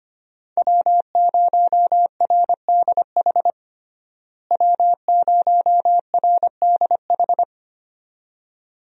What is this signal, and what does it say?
Morse code, heard as a keyed tone, about 25 wpm, 700 Hz. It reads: W0RDH W0RDH
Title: USA Call Signs - 5 Characters - Repeated 2x - 25wpm